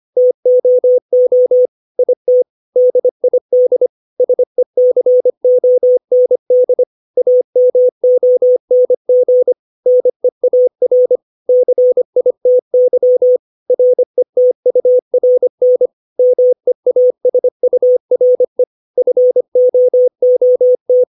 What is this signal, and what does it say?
Morse code, heard as a keyed tone, about 25 wpm, 500 Hz. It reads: TOO IT DID SECOND AMONG NEAR CITY RETURN MEASURE FOOT
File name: 632 words / 632 Words 25WPM